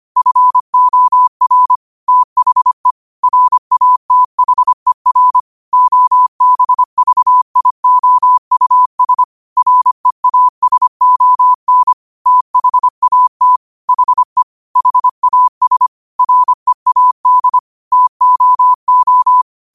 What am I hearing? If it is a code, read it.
Morse code, heard as a keyed tone, about 25 wpm, 1000 Hz. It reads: FOR THE RATHER OBVIOUS REASON THAT HE HAS READ TOO
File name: qotdfiles / 25wpm